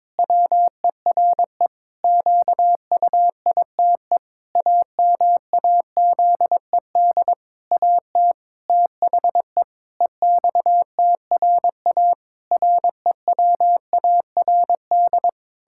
Morse code, heard as a keyed tone, about 22 wpm, 700 Hz. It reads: WERE QUITE AMAZED AT THE EXTRA REWARD
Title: Pangram Challenge v2 - 22wpm